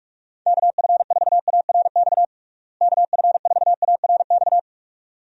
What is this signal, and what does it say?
Morse code, heard as a keyed tone, about 46 wpm, 700 Hz. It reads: KF4ARX KF4ARX
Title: USA Call Signs - 6 Characters - Repeated 2x - 45wpm